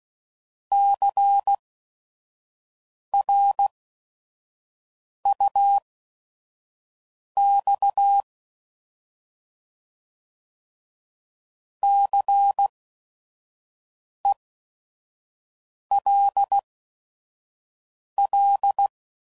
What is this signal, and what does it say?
Morse code, heard as a keyed tone, about 16 wpm, spaced as for 5 wpm, 800 Hz. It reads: CRUX CELL
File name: Prefixes / By First Letter / Cs